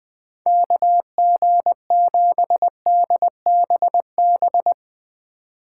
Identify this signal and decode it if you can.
Morse code, heard as a keyed tone, about 22 wpm, 700 Hz. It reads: KG7DBB
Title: Speed Racing - USA 2x3 Call Signs 20wpm